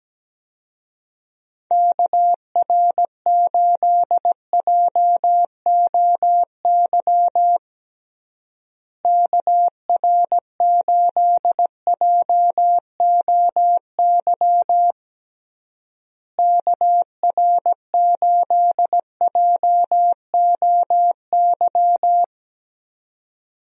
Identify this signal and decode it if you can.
Morse code, heard as a keyed tone, about 17 wpm, 700 Hz. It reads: KR8JOY KR8JOY KR8JOY
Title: USA Call Signs - 6 Characters - Repeated 3x - 17wpm